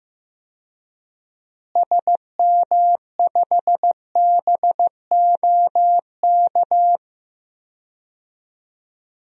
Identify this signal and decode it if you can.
Morse code, heard as a keyed tone, about 15 wpm, 700 Hz. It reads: SM5BOK
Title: DXCC-100 Call Signs - 6 Characters - Repeated 3x - 15wpm